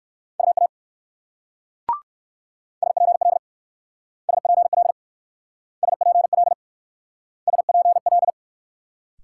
Morse code, heard as a keyed tone, about 55 wpm, 700 Hz. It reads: UA SPL SPL SPL SPL
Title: Speed-Racing Multi-letter Phonemes - 45wpm